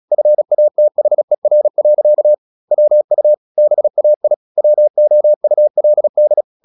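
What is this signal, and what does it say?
Morse code, heard as a keyed tone, about 36 wpm, 600 Hz. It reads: FATHER. WU BAI WOULD
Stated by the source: Morse practice at 36 wpm, text of Ken Follett's novel Never, from CW med Gnister.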